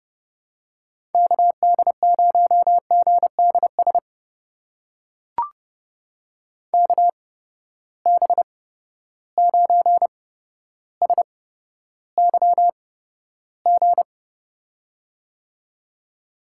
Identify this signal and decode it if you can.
Morse code, heard as a keyed tone, about 30 wpm, spaced as for 13 wpm, 700 Hz. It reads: KD0GDS KB9SYG
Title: USA Call Signs - 6 Characters - 8x Character Spacing - 30wpm